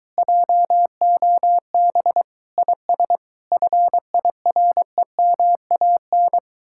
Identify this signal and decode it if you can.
Morse code, heard as a keyed tone, about 23 wpm, 700 Hz. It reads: JOB IS FIREMAN